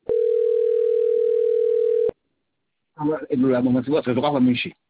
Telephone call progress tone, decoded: ringback tone